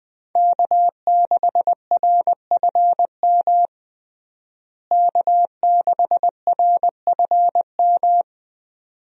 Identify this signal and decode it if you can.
Morse code, heard as a keyed tone, about 20 wpm, 700 Hz. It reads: K6RFM K6RFM
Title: USA Call Signs - 5 Characters - Repeated 2x - 20wpm